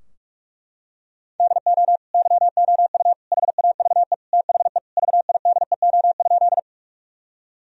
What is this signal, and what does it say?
Morse code, heard as a keyed tone, about 43 wpm, 700 Hz. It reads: DO YOU HAVE THE VIDEO?
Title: Sentences from Top 1000 Words - Review Part 4 - 45wpm